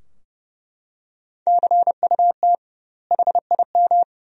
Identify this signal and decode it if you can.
Morse code, heard as a keyed tone, about 30 wpm, 700 Hz. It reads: CUT HIM
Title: 2x Word-Spacing - Sentences from Top 300 Words - 30wpm